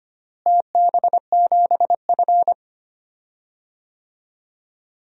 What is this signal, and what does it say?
Morse code, heard as a keyed tone, about 25 wpm, 700 Hz. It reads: TB7F
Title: DXCC-100 Call Signs - 4 Characters - Repeated 2x - 25wpm